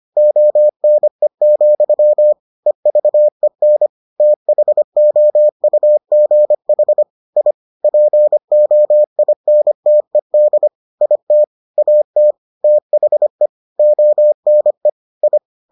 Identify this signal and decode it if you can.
Morse code, heard as a keyed tone, about 25 wpm, 600 Hz. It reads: ONE, EVEN THOUGH I POINTED IT AT THE ONE I